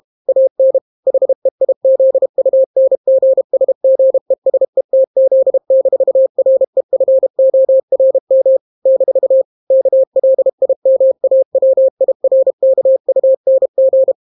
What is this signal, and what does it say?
Morse code, heard as a keyed tone, about 32 wpm, 530 Hz. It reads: AN HEIZUNGSGESETZ-REFORM - KLIMAWIRKUNG